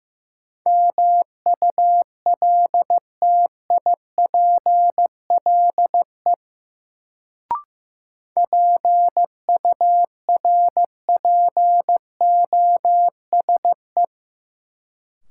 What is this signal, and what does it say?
Morse code, heard as a keyed tone, about 15 wpm, 700 Hz. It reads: MULTIPLE PURPOSE
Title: Sets of 5 Words - New Word Drill - 15wpm